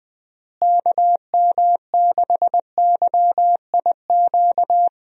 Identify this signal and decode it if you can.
Morse code, heard as a keyed tone, about 21 wpm, 700 Hz. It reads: KM6YIQ
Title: USA Call Signs - 6 Characters - Repeated 2x - 20wpm